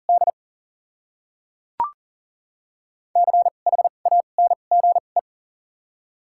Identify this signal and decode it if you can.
Morse code, heard as a keyed tone, about 40 wpm, spaced as for 29 wpm, 700 Hz. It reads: D CHANGE
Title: 2x Character Spacing - Top 200 Words - 40wpm